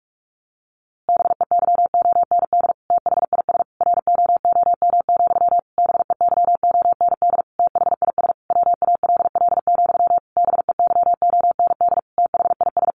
Morse code, heard as a keyed tone, about 44 wpm, 700 Hz. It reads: BEYOND THIS ROOM, BEYOND THIS WALL, BEYOND THIS
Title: A Sound Of Thunder 45wpm